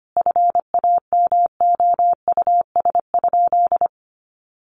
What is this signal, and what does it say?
Morse code, heard as a keyed tone, about 25 wpm, 700 Hz. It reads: FAMOUS?
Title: Sentences from Top 1500 Words 25wpm